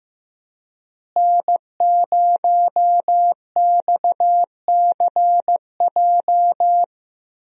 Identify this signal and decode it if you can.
Morse code, heard as a keyed tone, about 15 wpm, 700 Hz. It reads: N0XCJ
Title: USA Call Signs - 5 Characters - Repeated 3x - 15wpm